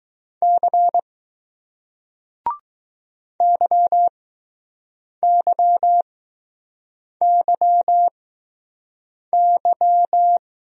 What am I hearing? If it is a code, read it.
Morse code, heard as a keyed tone, about 20 wpm, spaced as for 6 wpm, 700 Hz. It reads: C YYYY